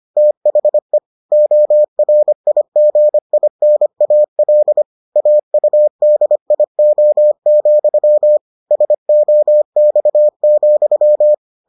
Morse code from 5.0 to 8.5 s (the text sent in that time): AUDIO,